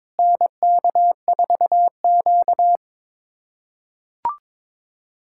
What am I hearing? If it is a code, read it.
Morse code, heard as a keyed tone, about 22 wpm, 700 Hz. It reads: NK4Q
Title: USA Callsign DE Callsign - 4 Characters - 22wpm